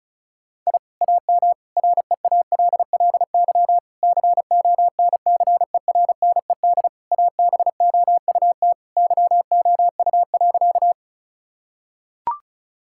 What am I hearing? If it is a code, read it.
Morse code, heard as a keyed tone, about 36 wpm, 700 Hz. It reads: I AM REALLY CONCERNED ABOUT YOU.
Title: Sentences from Top 1000 Words - Review Part 6 - 35wpm